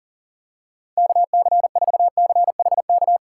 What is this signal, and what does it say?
Morse code, heard as a keyed tone, about 39 wpm, 700 Hz. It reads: KC4CHK
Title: USA Call Signs - 6 Characters - 2x Character Spacing - 40wpm